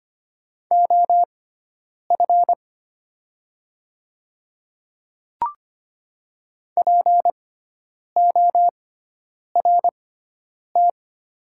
Morse code, heard as a keyed tone, about 25 wpm, spaced as for 9 wpm, 700 Hz. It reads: OF PORT
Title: 6x Character Spacing - Top 200 Words - 25wpm